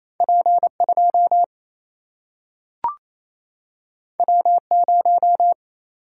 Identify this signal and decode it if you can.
Morse code, heard as a keyed tone, about 28 wpm, 700 Hz. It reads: P2 W0
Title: Single-Letter Single-Number 28wpm